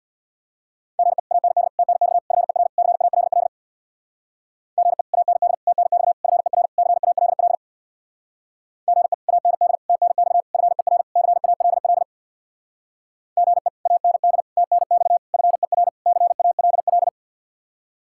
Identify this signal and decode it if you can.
Morse code, heard as a keyed tone, about 64 wpm, 700 Hz. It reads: GE AND TNX FER CALL GE AND TNX FER CALL GE AND TNX FER CALL GE AND TNX FER CALL